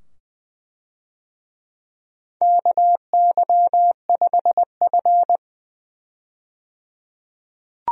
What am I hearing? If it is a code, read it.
Morse code, heard as a keyed tone, about 20 wpm, 700 Hz. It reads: KY5F
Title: USA Call Signs - 4 Characters - 20wpm